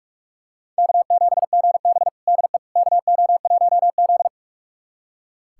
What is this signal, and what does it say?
Morse code, heard as a keyed tone, about 46 wpm, 700 Hz. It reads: K7GB DE KO1Z
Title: USA Callsign DE Callsign - 4 Characters - 45wpm